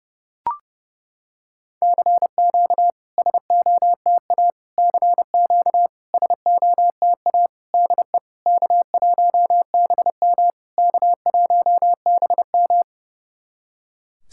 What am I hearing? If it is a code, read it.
Morse code, heard as a keyed tone, about 30 wpm, 690 Hz. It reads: CQ SOTA CQ SOTA DE K1BM K1BM
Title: SOTA QSO Practice 30wpm